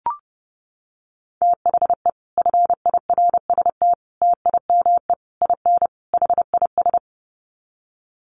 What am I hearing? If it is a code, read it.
Morse code, heard as a keyed tone, about 30 wpm, 700 Hz. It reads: THE FIRST TIME IN HIS
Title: Sets of 5 Words Encore2 30wpm